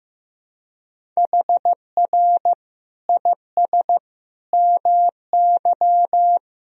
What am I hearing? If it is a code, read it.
Morse code, heard as a keyed tone, about 15 wpm, 700 Hz. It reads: HR IS MY
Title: Sentences from Top 200 Words and Abbreviations 15wpm